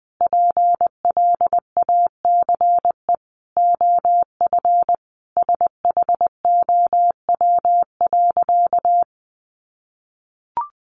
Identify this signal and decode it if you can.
Morse code, heard as a keyed tone, about 20 wpm, 700 Hz. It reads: PLACE OF SHOW.
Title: Sentences from Top 1400 Words 20wpm